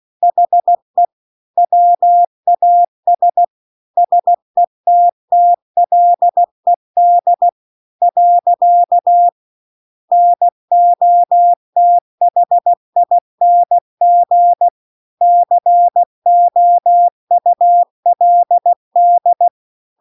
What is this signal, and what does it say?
Morse code, heard as a keyed tone, about 16 wpm, 700 Hz. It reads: HE WAS SETTLED . NOTHING COULD